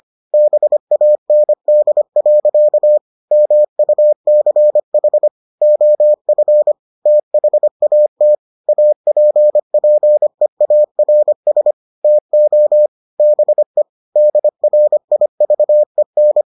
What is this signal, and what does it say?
Morse code, heard as a keyed tone, about 25 wpm, 600 Hz. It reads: BAND. MUCH OF THAT APPEARS TO BE DRIVEN